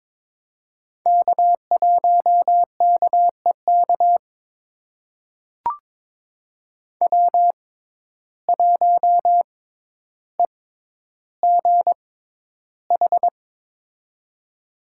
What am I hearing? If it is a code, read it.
Morse code, heard as a keyed tone, about 21 wpm, spaced as for 11 wpm, 700 Hz. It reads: K1KEK W1EGH
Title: USA Call Signs - 5 Characters - 6x Character Spacing - 22wpm